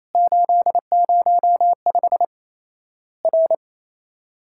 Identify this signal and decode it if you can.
Morse code, heard as a keyed tone, about 28 wpm, 700 Hz. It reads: 805 R
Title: Worked All Europe DX Contest - QTC Exchange - 28wpm